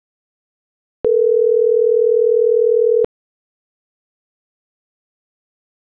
Ringback tone (North America), 440 + 480 Hz.